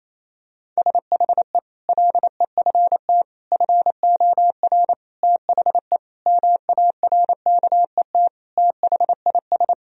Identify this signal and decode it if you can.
Morse code, heard as a keyed tone, about 28 wpm, 700 Hz. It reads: SHE LEFT FOR THE MARKET THIS
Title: Sentences from Top 1000 Words - Review Part 3 - 28wpm